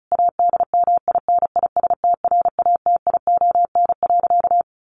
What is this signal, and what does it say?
Morse code, heard as a keyed tone, about 34 wpm, 700 Hz. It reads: ADMINISTRATION.
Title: Sentences from Top 1300 Words 35wpm